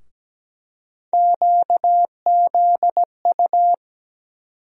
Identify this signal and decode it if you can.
Morse code, heard as a keyed tone, about 17 wpm, 700 Hz. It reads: QZU
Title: USA Call Sign Postfixes - 3 Letters 17wpm